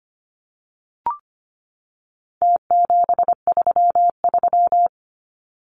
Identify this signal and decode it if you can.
Morse code, heard as a keyed tone, about 25 wpm, 700 Hz. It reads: T733
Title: Numbers - Contest Style - 4-Digit 25wpm